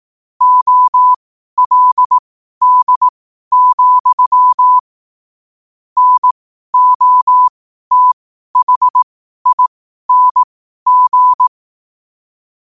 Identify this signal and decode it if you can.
Morse code, heard as a keyed tone, about 18 wpm, spaced as for 12 wpm, 1000 Hz. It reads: OLD, NOTHING
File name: qotdfiles / 12wpm